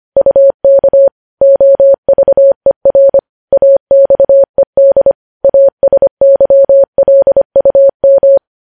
Morse code, heard as a keyed tone, about 25 wpm, 550 Hz. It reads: UK OVER AXED ASYLUM
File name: podcast / News headlines-1x-25x25